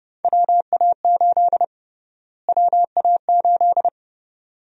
Morse code, heard as a keyed tone, about 31 wpm, 700 Hz. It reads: WA8 WA8
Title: USA Call Sign Prefixes - Repeated 2x - 30wpm